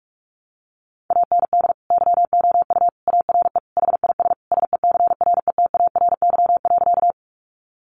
Morse code, heard as a keyed tone, about 44 wpm, 700 Hz. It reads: AND YOU ARE HIS SECRETARY.
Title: Sentences from Top 1300 Words 45wpm